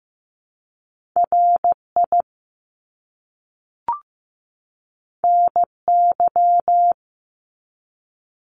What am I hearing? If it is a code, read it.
Morse code, heard as a keyed tone, about 15 wpm, 700 Hz. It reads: RI NY
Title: US States 15wpm